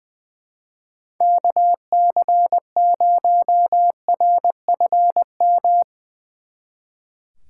Morse code, heard as a keyed tone, about 22 wpm, 700 Hz. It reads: KC0RFM
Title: Speed Racing - USA 2x3 Call Signs 20wpm